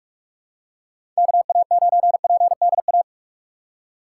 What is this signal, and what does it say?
Morse code, heard as a keyed tone, about 46 wpm, 700 Hz. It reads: KA9PDA
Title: USA Call Signs - 6 Characters - 3x Character Spacing - 45wpm